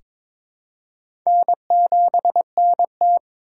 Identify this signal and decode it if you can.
Morse code, heard as a keyed tone, about 22 wpm, 700 Hz. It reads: N7NT
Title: K1USN SST QSO Practice - 22wpm